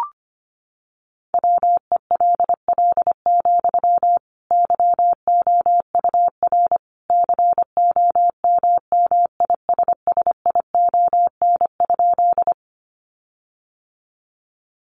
Morse code, heard as a keyed tone, about 25 wpm, 700 Hz. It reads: WELL, YOUR COMMISSION?
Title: Sentences from Top 1400 Words 25wpm